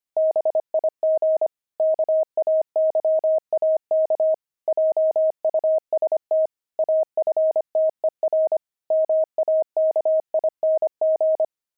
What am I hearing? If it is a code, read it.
Morse code, heard as a keyed tone, about 25 wpm, 630 Hz. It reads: BIG KAYAK JUST AFTER MAKING
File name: pangram-fs-25-cs-25-explorer